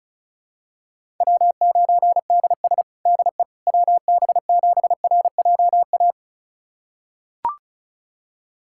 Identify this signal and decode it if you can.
Morse code, heard as a keyed tone, about 36 wpm, 700 Hz. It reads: W9DS DE WB7RJA
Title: USA Callsign DE Callsign 35wpm